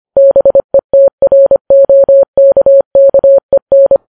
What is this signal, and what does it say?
Morse code, heard as a keyed tone, about 25 wpm, 550 Hz. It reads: BETROKKEN
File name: Files Web tls / Morse on9fv / morse25kps